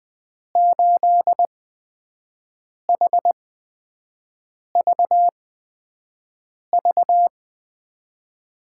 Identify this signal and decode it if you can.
Morse code, heard as a keyed tone, about 20 wpm, spaced as for 5 wpm, 700 Hz. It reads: 8HVV
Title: USA Call Signs - 6 Characters - 8x Character Spacing - 20wpm